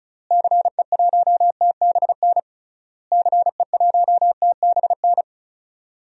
Morse code, heard as a keyed tone, about 35 wpm, 700 Hz. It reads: CE1TBN CE1TBN
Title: DXCC-100 Call Signs - 6 Characters - Repeated 3x - 35wpm